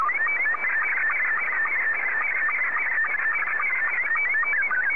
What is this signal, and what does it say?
SSTV (slow-scan television), Scottie 2.